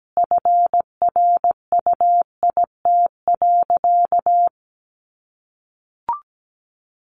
Morse code, heard as a keyed tone, about 17 wpm, 700 Hz. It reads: FRUIT.
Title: Sentences from Top 1300 Words 17wpm